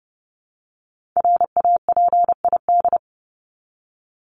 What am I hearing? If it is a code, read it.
Morse code, heard as a keyed tone, about 30 wpm, 700 Hz. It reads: RAPID
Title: LY Suffix - Top 100 Words with Base Words 30wpm